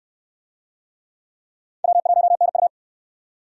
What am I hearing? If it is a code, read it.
Morse code, heard as a keyed tone, about 60 wpm, 700 Hz. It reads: W1NU